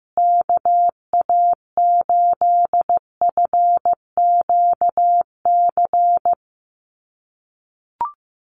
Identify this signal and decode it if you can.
Morse code, heard as a keyed tone, about 15 wpm, 700 Hz. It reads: KA8FQC
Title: USA Call Signs Repeated 3x 15wpm